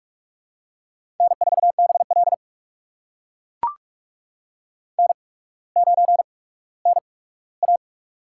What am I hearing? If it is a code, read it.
Morse code, heard as a keyed tone, about 44 wpm, spaced as for 19 wpm, 700 Hz. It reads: N4BL N9NA
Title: USA Call Signs - 4 Characters - 8x Character Spacing - 45wpm